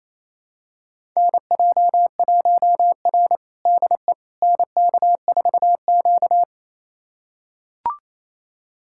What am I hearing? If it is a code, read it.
Morse code, heard as a keyed tone, about 28 wpm, 700 Hz. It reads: NJ1R DE NK4Q
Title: USA Callsign DE Callsign - 4 Characters - 28wpm